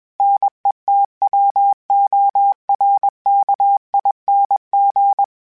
Morse code, heard as a keyed tone, about 21 wpm, 800 Hz. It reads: NETWORKING